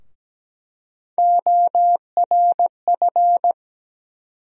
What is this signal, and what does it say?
Morse code, heard as a keyed tone, about 17 wpm, 700 Hz. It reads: ORF
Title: USA Call Sign Postfixes - 3 Letters 17wpm